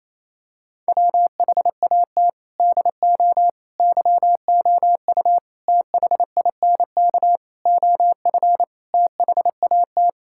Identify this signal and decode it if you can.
Morse code, heard as a keyed tone, about 28 wpm, 700 Hz. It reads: WHAT DO YOU THINK OF THAT
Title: Sentences from Top 1000 Words - Review Part 6 - 28wpm